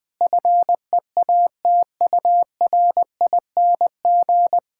Morse code, heard as a keyed tone, about 21 wpm, 700 Hz. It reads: FEATURING
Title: All-Age Appropriate Jokes - 20wpm